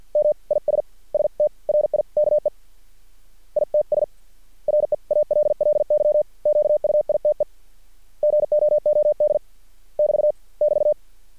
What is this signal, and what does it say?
Morse code, heard as a keyed tone, about 50 wpm, spaced as for 44 wpm, 600 Hz. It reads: M IS STRIKE ITS REALLY QUITE GOOD <BT> <BT>
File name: archived / CRSnet